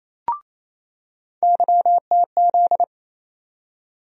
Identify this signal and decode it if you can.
Morse code, heard as a keyed tone, about 28 wpm, 700 Hz. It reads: YTZ
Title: USA Call Sign Postfixes - 3 Letters 28wpm